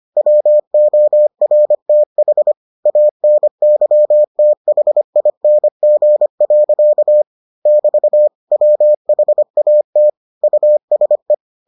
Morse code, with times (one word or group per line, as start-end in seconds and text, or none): 0.2-2.5 WORTH
2.9-7.2 ANYTHING.
7.7-8.3 <BT>
8.5-10.1 WHAT
10.4-11.3 USE